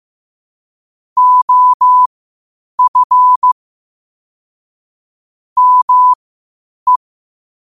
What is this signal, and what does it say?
Morse code, heard as a keyed tone, about 15 wpm, spaced as for 8 wpm, 1000 Hz. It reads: OF ME
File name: qotdfiles / 8wpm